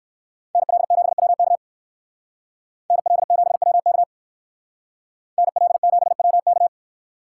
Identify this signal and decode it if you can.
Morse code, heard as a keyed tone, about 58 wpm, 700 Hz. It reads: NL7WK NL7WK NL7WK